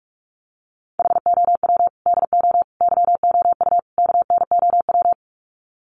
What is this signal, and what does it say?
Morse code, heard as a keyed tone, about 44 wpm, 700 Hz. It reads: HOW DO YOU KNOW
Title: Sets of 4 Words 45wpm